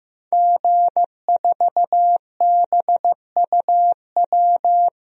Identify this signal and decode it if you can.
Morse code, heard as a keyed tone, about 16 wpm, 700 Hz. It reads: G4BUW
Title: DXCC-100 Call Signs - 5 Characters - Repeated 3x - 15wpm